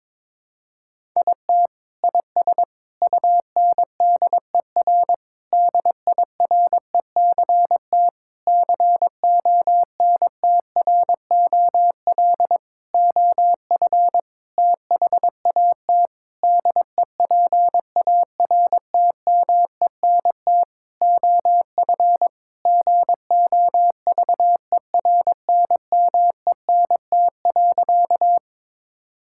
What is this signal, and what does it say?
Morse code, heard as a keyed tone, about 22 wpm, 700 Hz. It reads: IT IS UNDER DIRECT CONTROL OF THAT DEPARTMENT OF GOVERNMENT.
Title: Sentences from Top 1000 Words - Review Part 5 - 22wpm